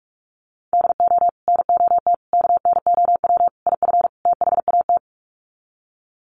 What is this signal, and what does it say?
Morse code, heard as a keyed tone, about 44 wpm, 700 Hz. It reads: DO NOT KNOW IF THAT
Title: Sets of 5 Words Encore2 45wpm